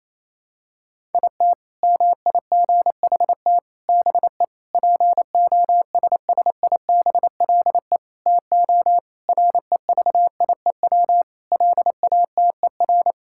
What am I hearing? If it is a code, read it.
Morse code, heard as a keyed tone, about 28 wpm, 700 Hz. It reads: IT MIGHT BE POSSIBLE TO REVIEW LATER
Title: Sentences from Top 1000 Words - Review Part 6 - 28wpm